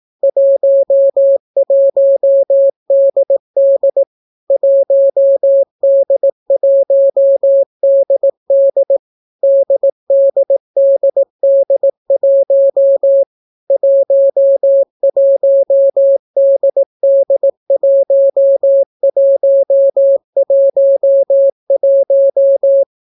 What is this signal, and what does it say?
Morse code, heard as a keyed tone, about 18 wpm, 550 Hz. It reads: 11DD 1D1DD DDDD1 11DD1111